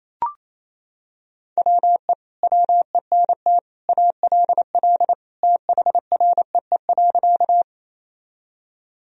Morse code, heard as a keyed tone, about 28 wpm, 700 Hz. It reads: WE WENT ALL THREE.
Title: Sentences from Top 1000 Words - Review Part 1 - 28wpm